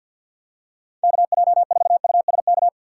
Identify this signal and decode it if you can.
Morse code, heard as a keyed tone, about 46 wpm, 700 Hz. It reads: KJ4USK
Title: USA Call Signs - 6 Characters - 6x Character Spacing - 50wpm